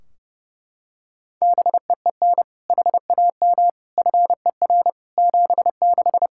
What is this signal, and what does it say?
Morse code, heard as a keyed tone, about 32 wpm, 700 Hz. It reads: BEEN HAM FER 76
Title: K1USN SST QSO Practice 30wpm